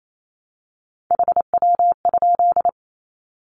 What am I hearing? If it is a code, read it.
Morse code, heard as a keyed tone, about 28 wpm, 700 Hz. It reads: HW?
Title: Common QSO Phrases 28wpm